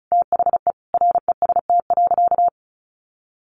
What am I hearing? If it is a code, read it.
Morse code, heard as a keyed tone, about 34 wpm, 700 Hz. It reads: THE REST.
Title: Sentences from Top 1400 Words 35wpm